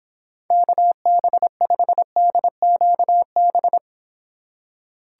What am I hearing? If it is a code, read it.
Morse code, heard as a keyed tone, about 26 wpm, 700 Hz. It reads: KB5DQB